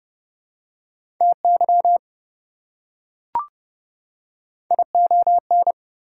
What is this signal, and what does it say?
Morse code, heard as a keyed tone, about 30 wpm, 700 Hz. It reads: TY ION
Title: Prefixes and Suffixes 30wpm